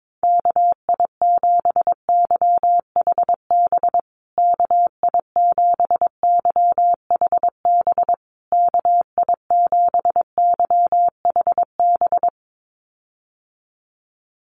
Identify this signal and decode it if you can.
Morse code, heard as a keyed tone, about 22 wpm, 700 Hz. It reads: KI7YHB KI7YHB KI7YHB
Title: USA Call Signs Repeated 3x 22wpm